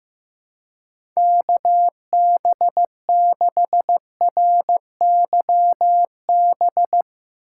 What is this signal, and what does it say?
Morse code, heard as a keyed tone, about 15 wpm, 700 Hz. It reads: KB6RYB
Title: USA Call Signs - 6 Characters - 3x Character Spacing - 15wpm